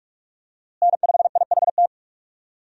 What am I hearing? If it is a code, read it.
Morse code, heard as a keyed tone, about 41 wpm, 700 Hz. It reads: N5IHT
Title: USA Call Signs - 5 Characters - Repeated 2x - 45wpm